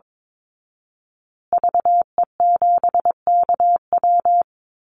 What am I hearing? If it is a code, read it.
Morse code, heard as a keyed tone, about 22 wpm, 700 Hz. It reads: VE7KW
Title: Call Signs Repeated 2x - 22wpm